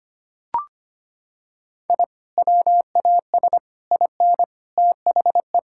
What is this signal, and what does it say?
Morse code, heard as a keyed tone, about 25 wpm, 700 Hz. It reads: I WAS IN THE
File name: practice / 25wpm / Sentences with 3-Word Phrases - 25wpm